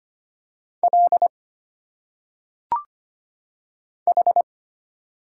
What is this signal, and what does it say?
Morse code, heard as a keyed tone, about 25 wpm, 700 Hz. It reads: L H
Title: Single Letters - Phonetic Pronunciation 25wpm